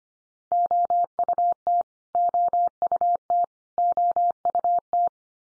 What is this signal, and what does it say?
Morse code, heard as a keyed tone, about 25 wpm, 700 Hz. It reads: OUT OUT OUT
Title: Top 100 Words 25wpm x3